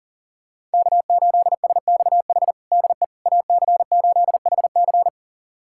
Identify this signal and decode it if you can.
Morse code, heard as a keyed tone, about 40 wpm, 700 Hz. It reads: K8SXH DE AC8HC
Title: USA Callsign DE Callsign 40wpm